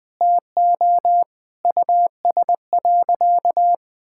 Morse code, heard as a keyed tone, about 20 wpm, 700 Hz. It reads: TO US.
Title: Sentences from Top 1600 Words 20wpm